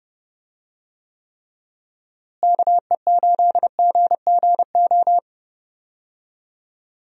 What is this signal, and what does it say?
Morse code, heard as a keyed tone, about 30 wpm, 700 Hz. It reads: KE8GGO
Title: USA Call Signs - 6 Characters - Repeated 2x - 30wpm